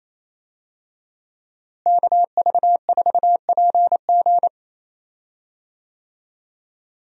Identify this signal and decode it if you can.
Morse code, heard as a keyed tone, about 28 wpm, 700 Hz. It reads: KV4PG
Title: USA Call Signs - 5 Characters - Repeated 3x - 28wpm